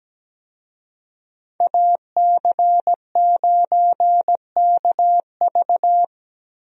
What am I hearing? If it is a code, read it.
Morse code, heard as a keyed tone, about 17 wpm, 700 Hz. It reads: AC9KV
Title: USA Call Signs - 5 Characters - Repeated 2x - 17wpm